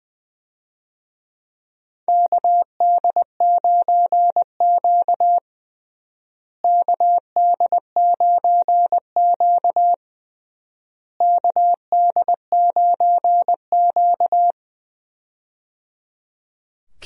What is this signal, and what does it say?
Morse code, heard as a keyed tone, about 21 wpm, 700 Hz. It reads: KD9Q KD9Q KD9Q
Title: USA Call Signs - 4 Characters - Repeated 3x - 20wpm